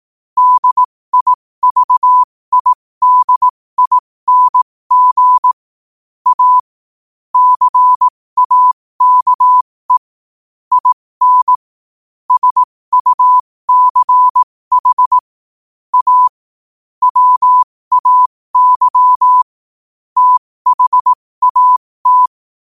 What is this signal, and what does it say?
Morse code, heard as a keyed tone, about 18 wpm, spaced as for 15 wpm, 1000 Hz. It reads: DIVIDING A CAKE IN SUCH A WAY THAT
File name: qotdfiles / 15wpm